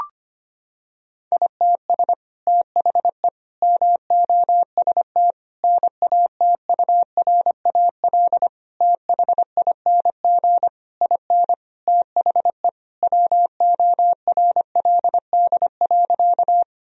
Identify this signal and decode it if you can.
Morse code, heard as a keyed tone, about 25 wpm, 700 Hz. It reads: ITS THE MOST NATURAL THING IN THE WORLD.
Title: Sentences from Top 1000 Words - Review Part 4 - 25wpm